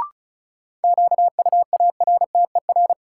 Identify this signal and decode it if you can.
Morse code, heard as a keyed tone, about 36 wpm, 700 Hz. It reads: QUARTER
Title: Sentences from Top 1300 Words - New Word Drill Alt - 35wpm